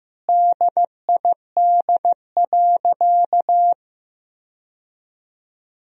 Morse code, heard as a keyed tone, about 15 wpm, 700 Hz. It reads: DID.
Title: 4x Word-Spacing - Sentences from Top 100 Words - 15wpm